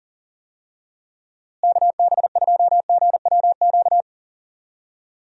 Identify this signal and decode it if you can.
Morse code, heard as a keyed tone, about 40 wpm, 700 Hz. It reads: KB2GWQ
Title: USA Call Signs - 6 Characters - Repeated 3x - 40wpm